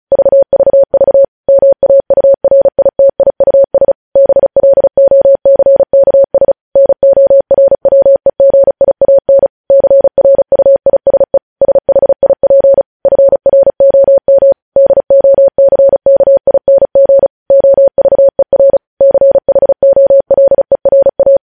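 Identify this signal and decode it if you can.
Morse code, heard as a keyed tone, about 36 wpm, 550 Hz. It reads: VVV MAURITIUS BLOCKS NORWEGIAN CRUISE SHIP FROM DOCKING OVER CHOLERA
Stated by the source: Morse practice sending news headlines at 35 wpm.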